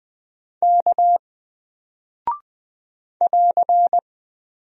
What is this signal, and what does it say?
Morse code, heard as a keyed tone, about 20 wpm, 700 Hz. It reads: K <AR>
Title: Single Character (Letter:Number:Punctuation:Prosign) Alt 20wpm